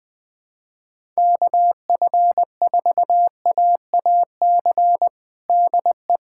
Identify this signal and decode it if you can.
Morse code, heard as a keyed tone, about 20 wpm, 700 Hz. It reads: KF4AAC DE
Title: USA Callsign DE Callsign 20wpm